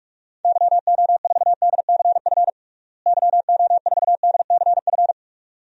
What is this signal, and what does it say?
Morse code, heard as a keyed tone, about 46 wpm, 700 Hz. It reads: YO4DCF YO4DCF
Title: DXCC-100 Call Signs - 6 Characters - Repeated 2x - 45wpm